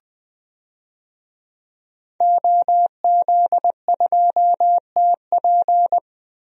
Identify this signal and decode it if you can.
Morse code, heard as a keyed tone, about 21 wpm, 700 Hz. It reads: OZ2TP
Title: DXCC-100 Call Signs - 5 Characters - 20wpm